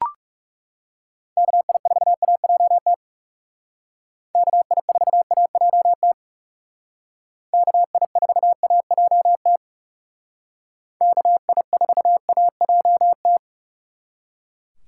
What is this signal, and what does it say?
Morse code, heard as a keyed tone, about 37 wpm, 700 Hz. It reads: KI4AJT KI4AJT KI4AJT KI4AJT
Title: Speed Racing - USA Call Signs 30wpm